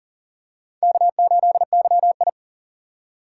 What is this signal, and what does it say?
Morse code, heard as a keyed tone, about 40 wpm, 700 Hz. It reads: K8YI
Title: USA Call Signs - 4 Characters - 3x Character Spacing - 40wpm